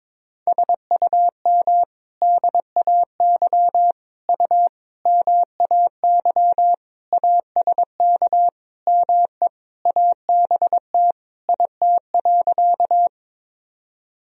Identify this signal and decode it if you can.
Morse code, heard as a keyed tone, about 22 wpm, 700 Hz. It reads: SUM DAY U MAY ASK ME ABT IT.
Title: Sentences from Top 200 Words and Abbreviations 22wpm